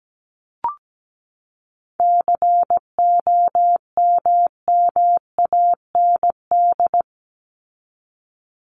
Morse code, heard as a keyed tone, about 17 wpm, 700 Hz. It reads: COMMAND
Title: Top 1501-1750 Words 17wpm